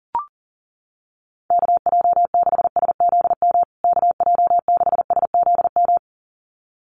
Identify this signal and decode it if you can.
Morse code, heard as a keyed tone, about 40 wpm, 700 Hz. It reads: KJ6SZM KJ6SZM
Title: USA Call Signs Repeated 2x 40wpm